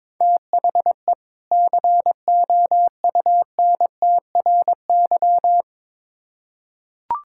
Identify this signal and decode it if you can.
Morse code, heard as a keyed tone, about 22 wpm, 700 Hz. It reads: THE COUNTRY
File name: Build A Phrase - 2-Word Phrases - 22wpm